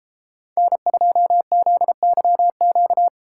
Morse code, heard as a keyed tone, about 33 wpm, 700 Hz. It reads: N2ZYQ